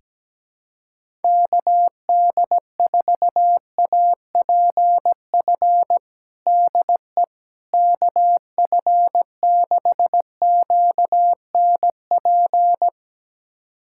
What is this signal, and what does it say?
Morse code, heard as a keyed tone, about 17 wpm, 700 Hz. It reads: KD4APF DE KF6QNP
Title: USA Callsign DE Callsign - 6 Characters - 17wpm